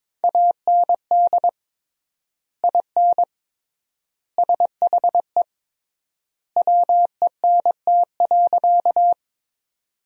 Morse code, heard as a keyed tone, about 22 wpm, 700 Hz. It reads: AND IN SHE WENT.
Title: 3x Word-Spacing - Sentences from Top 200 Words - 22wpm